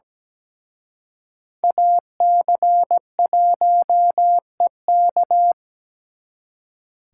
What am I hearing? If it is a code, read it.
Morse code, heard as a keyed tone, about 20 wpm, 700 Hz. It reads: AC1EK
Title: Speed Racing - USA Call Signs 17wpm